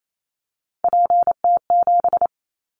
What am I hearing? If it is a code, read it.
Morse code, heard as a keyed tone, about 28 wpm, 700 Hz. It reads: PT7
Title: Callsign Prefixes for Top 50 DXCC - 28wpm